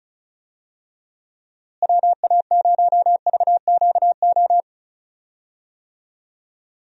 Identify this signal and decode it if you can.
Morse code, heard as a keyed tone, about 36 wpm, 700 Hz. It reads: WA0VQO
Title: USA Call Signs - 6 Characters - Repeated 3x - 35wpm